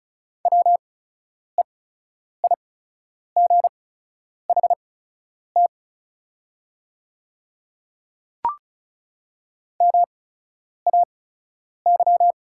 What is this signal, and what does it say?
Morse code, heard as a keyed tone, about 34 wpm, spaced as for 10 wpm, 700 Hz. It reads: WEIGHT MAY
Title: 8x Character Spacing - Top 500 Words - 35wpm